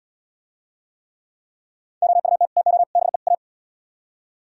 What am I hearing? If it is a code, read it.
Morse code, heard as a keyed tone, about 77 wpm, 700 Hz. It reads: OUT TO BE A